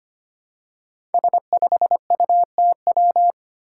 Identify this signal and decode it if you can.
Morse code, heard as a keyed tone, about 25 wpm, 700 Hz. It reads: S5UTW
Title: Five Challenging Letter-Number Combinations 25wpm